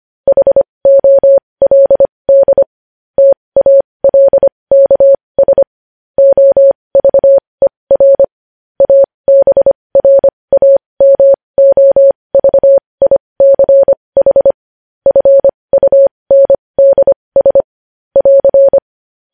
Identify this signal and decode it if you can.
Morse code, heard as a keyed tone, about 25 wpm, spaced as for 20 wpm, 550 Hz. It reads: HOLD TALKS OVER ABRAMOVICH FUNDS <AR>
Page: news-headlines-in-morse-code-at-20-wpm